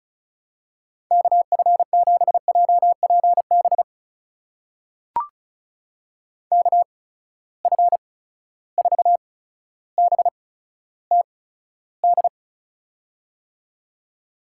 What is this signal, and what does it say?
Morse code, heard as a keyed tone, about 36 wpm, spaced as for 15 wpm, 700 Hz. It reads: KF7JPB KF4BTD
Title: USA Call Signs - 6 Characters - 8x Character Spacing - 35wpm